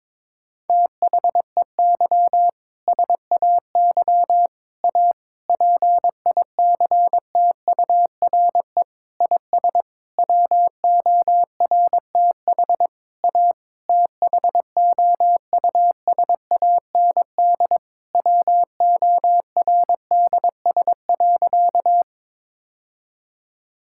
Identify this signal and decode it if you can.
Morse code, heard as a keyed tone, about 21 wpm, 700 Hz. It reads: THEY SAY A PICTURE IS WORTH A THOUSAND WORDS.
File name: Sentences with 7-Word Phrases - 22wpm